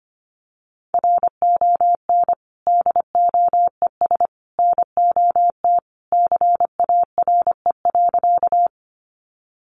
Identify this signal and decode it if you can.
Morse code, heard as a keyed tone, about 25 wpm, 700 Hz. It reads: RON DOES NOT CARE.
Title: Additional Easy Phrases 25wpm